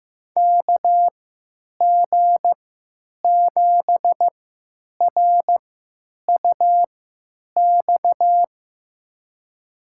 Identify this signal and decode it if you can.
Morse code, heard as a keyed tone, about 15 wpm, spaced as for 8 wpm, 700 Hz. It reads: KG7RUX
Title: USA Call Signs - 6 Characters - 3x Character Spacing - 15wpm